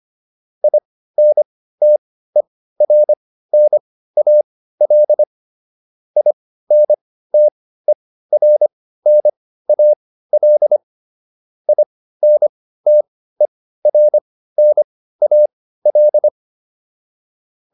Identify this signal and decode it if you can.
Morse code, heard as a keyed tone, about 25 wpm, spaced as for 15 wpm, 600 Hz. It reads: INTERNAL INTERNAL INTERNAL
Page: intermediate-practice-files